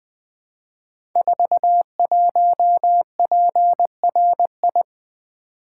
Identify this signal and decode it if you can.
Morse code, heard as a keyed tone, about 20 wpm, 700 Hz. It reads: 41PRI
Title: Code Groups - 5 Letter:Numbers - 20wpm